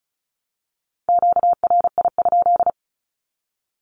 Morse code, heard as a keyed tone, about 34 wpm, 700 Hz. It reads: QRI?
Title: Q-Codes 35wpm